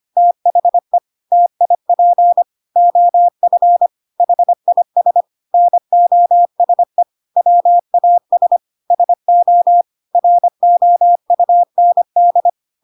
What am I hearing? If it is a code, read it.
Morse code, heard as a keyed tone, about 25 wpm, 700 Hz. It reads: THE TIP OF HIS NOSE WAS SO ROUND
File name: Pinocchio / Pinocchio ch1 25 25 ik1hge